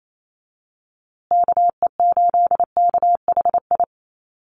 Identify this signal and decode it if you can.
Morse code, heard as a keyed tone, about 28 wpm, 700 Hz. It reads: KE8KHI
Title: USA Call Signs Repeated 2x 28wpm